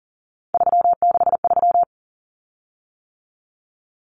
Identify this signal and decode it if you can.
Morse code, heard as a keyed tone, about 37 wpm, 700 Hz. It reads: 363
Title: Common Cut Numbers - 3-Digit 40wpm